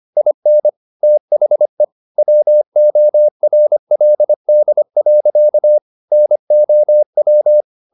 Morse code, heard as a keyed tone, about 25 wpm, 600 Hz. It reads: IN THE WORLD. NOW
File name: the old man and the sea part 1 25wpm